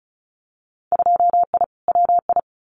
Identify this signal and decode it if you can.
Morse code, heard as a keyed tone, about 34 wpm, 700 Hz. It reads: 2I WI
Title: Winter Field Day Exchanges 35wpm